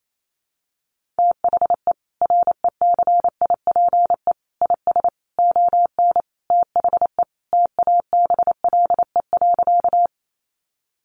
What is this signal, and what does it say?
Morse code, heard as a keyed tone, about 28 wpm, 700 Hz. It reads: THE RECIPE IS ON THE TABLE.
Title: Sentences from Top 1200 Words 28wpm